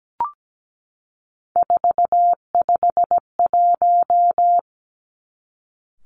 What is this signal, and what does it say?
Morse code, heard as a keyed tone, about 17 wpm, 700 Hz. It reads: 451
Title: Three Challenging Letter-Number Combinations 17wpm